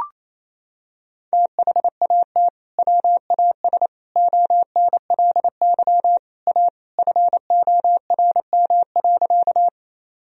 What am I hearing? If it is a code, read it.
Morse code, heard as a keyed tone, about 28 wpm, 700 Hz. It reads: THAT WAS ONLY A FORM.
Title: Sentences from Top 1000 Words - Review Part 1 - 28wpm